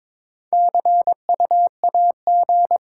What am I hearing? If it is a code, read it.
Morse code, heard as a keyed tone, about 22 wpm, 700 Hz. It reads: C U A G
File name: Top 20 QSO Elements - 8x Character Spacing - 22wpm